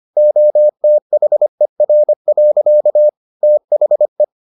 Morse code, heard as a keyed tone, about 25 wpm, 600 Hz. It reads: OTHER. THE